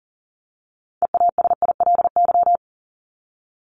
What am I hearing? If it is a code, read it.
Morse code, heard as a keyed tone, about 40 wpm, 700 Hz. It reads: EASILY
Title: Top 751-1000 Words 40wpm